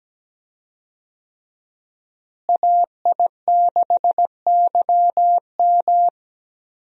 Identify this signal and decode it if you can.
Morse code, heard as a keyed tone, about 17 wpm, 700 Hz. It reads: AI6YM
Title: USA Call Signs - 5 Characters - Repeated 3x - 17wpm